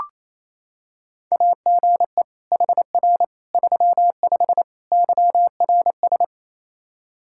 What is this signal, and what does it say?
Morse code, heard as a keyed tone, about 28 wpm, 700 Hz. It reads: AGE HR 35 YRS
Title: Practice QSO - Exchange 3 - More Info - 28wpm